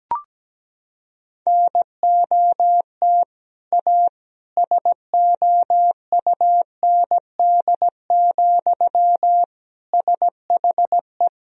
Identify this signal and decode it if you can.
Morse code, heard as a keyed tone, about 17 wpm, 700 Hz. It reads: NOT A SOUND, SHE
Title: Sentences from Top 1000 Words - Review Part 1 - 17wpm